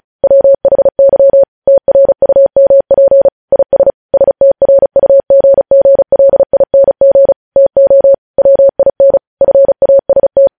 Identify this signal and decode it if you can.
Morse code, heard as a keyed tone, about 36 wpm, 550 Hz. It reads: WHY TRUMP IS STRUGGLING TO WIN FAST
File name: podcast / News headlines-1x-35x35